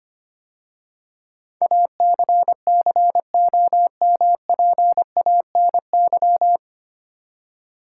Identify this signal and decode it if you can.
Morse code, heard as a keyed tone, about 25 wpm, 700 Hz. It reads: ACCOMPANY
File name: practice / 25wpm / 9 Letter Words - First 3 Letters Given - 25wpm